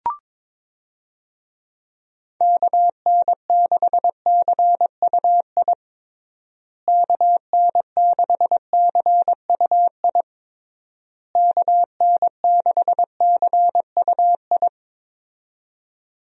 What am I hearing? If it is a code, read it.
Morse code, heard as a keyed tone, about 22 wpm, 700 Hz. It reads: KN6CUI KN6CUI KN6CUI
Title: USA Call Signs - 6 Characters - Repeated 3x - 22wpm